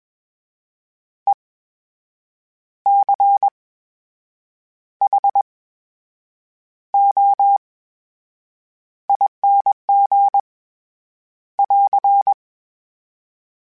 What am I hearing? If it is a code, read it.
Morse code, heard as a keyed tone, about 21 wpm, spaced as for 7 wpm, 800 Hz. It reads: ECHOING <AR>
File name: Special INGs-mp3s / Special INGs-mp3s / VerySlowToFaster Ing @20